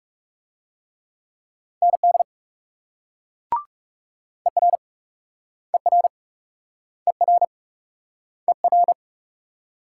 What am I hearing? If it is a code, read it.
Morse code, heard as a keyed tone, about 38 wpm, 700 Hz. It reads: ND ER ER ER ER